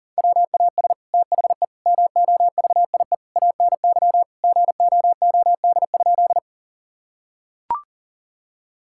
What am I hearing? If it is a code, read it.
Morse code, heard as a keyed tone, about 40 wpm, 700 Hz. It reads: WAS THE MOVIE ANY GOOD?
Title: Sentences from Top 1000 Words - Review Part 5 - 40wpm